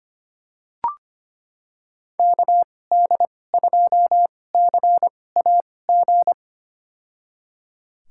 Morse code, heard as a keyed tone, about 25 wpm, spaced as for 18 wpm, 700 Hz. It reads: KD2CAG
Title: USA Call Signs - 6 Characters - 2x Character Spacing - 25wpm